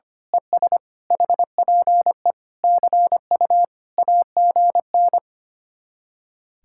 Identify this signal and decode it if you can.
Morse code, heard as a keyed tone, about 24 wpm, 700 Hz. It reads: ES HPE CU AGN
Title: Speed-Racing Common QSO Phrases - 25wpm